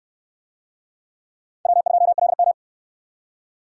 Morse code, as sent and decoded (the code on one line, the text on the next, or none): .-- ..--- .-.. --
W2LM